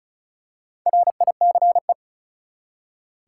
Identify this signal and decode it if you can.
Morse code, heard as a keyed tone, about 36 wpm, 700 Hz. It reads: RICE
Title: Four Letter Words - 35wpm